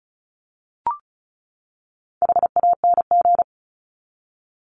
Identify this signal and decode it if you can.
Morse code, heard as a keyed tone, about 34 wpm, 700 Hz. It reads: HANG
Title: Top 1501-1750 Words 35wpm